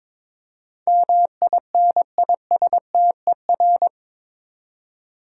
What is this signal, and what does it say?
Morse code, heard as a keyed tone, about 22 wpm, 700 Hz. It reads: MINISTER
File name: practice / 22wpm / Eight Letter Words - 22wpm